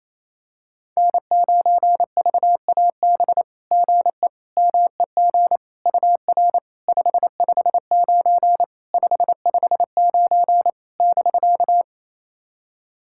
Morse code, heard as a keyed tone, about 27 wpm, 690 Hz. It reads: N9VAB GE MEG UR 559 559 <BK>
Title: SOTA QSO Practice 28wpm